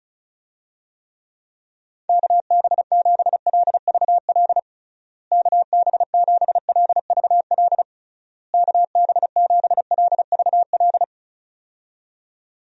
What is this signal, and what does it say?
Morse code, heard as a keyed tone, about 36 wpm, 700 Hz. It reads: KB7LVL KB7LVL KB7LVL
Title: USA Call Signs - 6 Characters - Repeated 3x - 35wpm